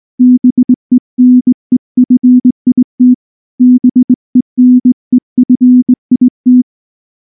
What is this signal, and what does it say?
Morse code, heard as a keyed tone, about 20 wpm, 250 Hz. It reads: BENEFIT BENEFIT
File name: morse / morse-training-20WPM-4